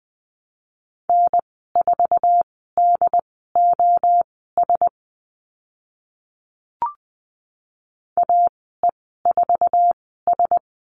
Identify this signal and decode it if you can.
Morse code, heard as a keyed tone, about 20 wpm, spaced as for 15 wpm, 700 Hz. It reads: N4DOS AE4S
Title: 069 Lesson 12g - Letter d - Call Signs 20wpm